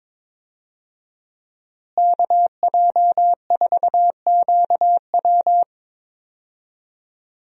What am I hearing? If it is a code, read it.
Morse code, heard as a keyed tone, about 22 wpm, 700 Hz. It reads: KJ4QW
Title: USA Call Signs - 5 Characters - Repeated 3x - 22wpm